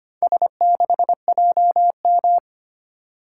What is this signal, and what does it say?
Morse code, heard as a keyed tone, about 25 wpm, 700 Hz. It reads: S6JM
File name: practice / 25wpm / Four Letter-Number Combinations - 25wpm